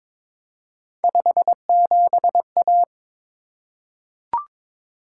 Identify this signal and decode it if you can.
Morse code, heard as a keyed tone, about 22 wpm, 700 Hz. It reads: 57A
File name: practice / 22wpm / Three Letter-Number Combinations - 22wpm